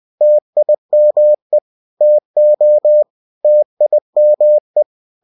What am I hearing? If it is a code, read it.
Morse code, heard as a keyed tone, about 20 wpm, 600 Hz. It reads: TIME TO TIME
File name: cw-books / winnie-the-pooh / winnie-20wpm-page-0028